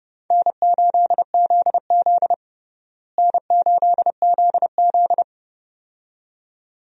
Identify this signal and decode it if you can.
Morse code, heard as a keyed tone, about 30 wpm, 700 Hz. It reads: N8ZZ N8ZZ
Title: USA Call Signs - 4 Characters - Repeated 2x - 30wpm